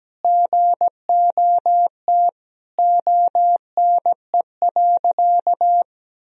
Morse code, heard as a keyed tone, about 17 wpm, 700 Hz. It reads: GOT ONE.
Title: Sentences from Top 300 Words and Abbreviations 17wpm